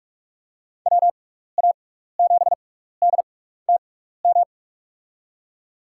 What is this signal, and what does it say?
Morse code, heard as a keyed tone, about 43 wpm, 700 Hz. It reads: W A 7 D T M
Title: USA Call Signs - 6 Characters - 6x Character Spacing - 45wpm